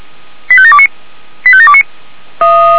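Five-tone selective call (ZVEI), heard as quick runs of five tones, tones of about 70 ms. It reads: 85729 85729
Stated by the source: ZVEI five-tone selective call received by SDR.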